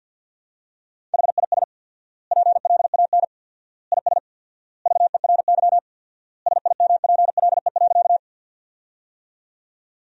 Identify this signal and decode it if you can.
Morse code, heard as a keyed tone, about 47 wpm, 700 Hz. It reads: HIS PLAN IS VERY SIMPLE.
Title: 4x Word-Spacing - Sentences from Top 500 Words - 50wpm